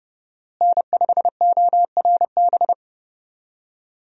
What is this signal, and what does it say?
Morse code, heard as a keyed tone, about 30 wpm, 700 Hz. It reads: N5ORB
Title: USA Call Signs - 5 Characters - Repeated 2x - 30wpm